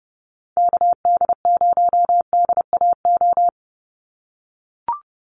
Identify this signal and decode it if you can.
Morse code, heard as a keyed tone, about 30 wpm, 700 Hz. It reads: KD0DAO
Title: USA Call Signs 30wpm